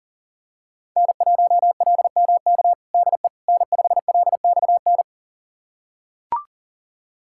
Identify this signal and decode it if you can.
Morse code, heard as a keyed tone, about 40 wpm, 690 Hz. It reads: N1LMK DE N5LXN
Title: USA Callsign DE Callsign - 5 Characters - 40wpm